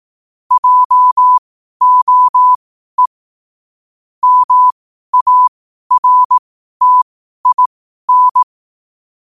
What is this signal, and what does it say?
Morse code, heard as a keyed tone, about 18 wpm, spaced as for 12 wpm, 1000 Hz. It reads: JOE MARTIN
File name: qotdfiles / 12wpm